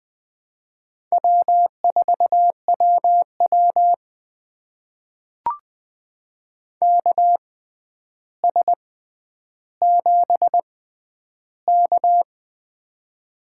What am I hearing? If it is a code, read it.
Morse code, heard as a keyed tone, about 20 wpm, spaced as for 10 wpm, 700 Hz. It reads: W4WW KS7K
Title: USA Call Signs - 4 Characters - 6x Character Spacing - 20wpm